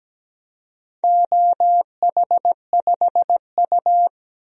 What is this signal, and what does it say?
Morse code, heard as a keyed tone, about 17 wpm, 700 Hz. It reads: OH5U
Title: DXCC-100 Call Signs - 4 Characters - 17wpm